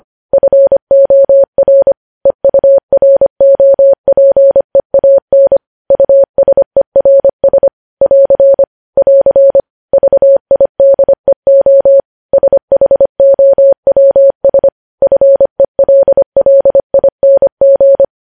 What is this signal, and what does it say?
Morse code, heard as a keyed tone, about 25 wpm, 550 Hz. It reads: FOR EUROPEAN USERS <AR> <AR> VIDEO SHOWS FELLING
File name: podcast / News headlines-1x-25x25